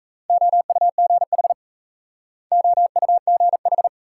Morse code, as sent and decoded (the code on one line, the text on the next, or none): --- ..- --. .... / --- ..- --. ....
OUGH OUGH